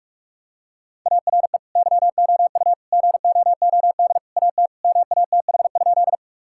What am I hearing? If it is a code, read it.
Morse code, heard as a keyed tone, about 46 wpm, 700 Hz. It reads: ARE YOU GOOD AT MATH?
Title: Sentences from Top 1000 Words - Review Part 6 - 45wpm